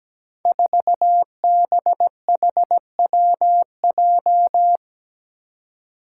Morse code, heard as a keyed tone, about 17 wpm, 700 Hz. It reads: 4BHWJ
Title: Five Challenging Letter-Number Combinations 17wpm